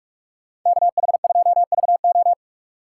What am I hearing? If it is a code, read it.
Morse code, heard as a keyed tone, about 47 wpm, 700 Hz. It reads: KH2VO
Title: USA Call Signs - 5 Characters - 6x Character Spacing - 45wpm